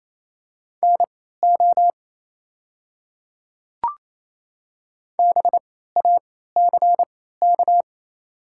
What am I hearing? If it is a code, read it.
Morse code, heard as a keyed tone, about 27 wpm, spaced as for 16 wpm, 690 Hz. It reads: NO BACK
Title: 3x Character Spacing - Top 300 Words - 28wpm